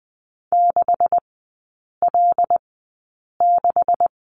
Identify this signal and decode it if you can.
Morse code, heard as a keyed tone, about 20 wpm, spaced as for 8 wpm, 700 Hz. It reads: 6L6
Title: 127 Lesson 21e - Number 6 - Three Characters 20wpm